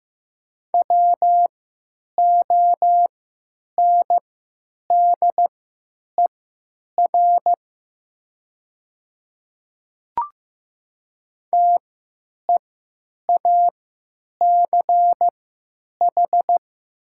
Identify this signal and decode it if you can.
Morse code, heard as a keyed tone, about 15 wpm, spaced as for 9 wpm, 700 Hz. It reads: WONDER TEACH
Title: 3x Character Spacing - Top 500 Words - 15wpm